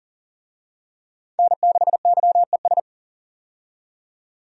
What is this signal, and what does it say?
Morse code, heard as a keyed tone, about 40 wpm, 700 Hz. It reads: N6YES
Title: USA Call Signs - 5 Characters - 40wpm